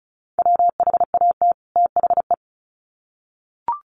Morse code, heard as a keyed tone, about 34 wpm, 700 Hz. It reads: WHAT THE
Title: Sets of 2 Words 35wpm